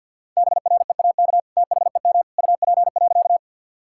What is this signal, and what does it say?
Morse code, heard as a keyed tone, about 50 wpm, 700 Hz. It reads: BREAK THEM UP.
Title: Sentences from Top 1000 Words - Review Part 6 - 50wpm